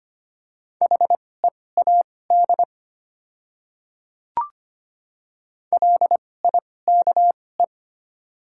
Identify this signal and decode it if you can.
Morse code, heard as a keyed tone, about 25 wpm, spaced as for 18 wpm, 700 Hz. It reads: HEAD LIKE
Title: 2x Character Spacing - Top 500 Words - 25wpm